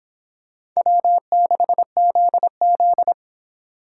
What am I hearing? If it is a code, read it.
Morse code, heard as a keyed tone, about 26 wpm, 700 Hz. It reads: W6ZZ